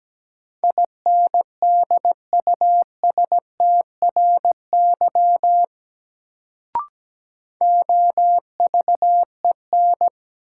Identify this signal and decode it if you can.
Morse code, heard as a keyed tone, about 17 wpm, 700 Hz. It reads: INDUSTRY OVEN
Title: Sentences from Top 1000 Words - Review Part 4 - New Word Drill - 17wpm